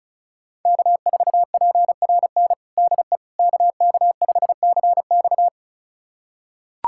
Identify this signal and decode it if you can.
Morse code, heard as a keyed tone, about 34 wpm, 700 Hz. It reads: K4PRN DE KK5CX
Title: USA Callsign DE Callsign - 5 Characters - 35wpm